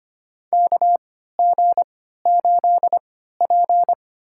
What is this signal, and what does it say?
Morse code, heard as a keyed tone, about 25 wpm, spaced as for 14 wpm, 700 Hz. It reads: KG8P
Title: USA Call Signs - 4 Characters - 3x Character Spacing - 25wpm